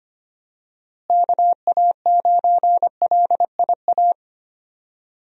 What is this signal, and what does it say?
Morse code, heard as a keyed tone, about 25 wpm, 700 Hz. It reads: KA9LIA
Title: USA Call Signs - 6 Characters - Repeated 2x - 25wpm